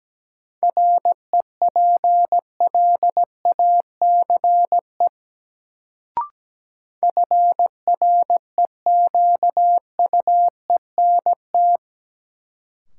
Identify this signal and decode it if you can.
Morse code, heard as a keyed tone, about 17 wpm, 700 Hz. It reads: REPLACE FREQUENT
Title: Sentences from Top 1400 Words - New Word Drill - 17wpm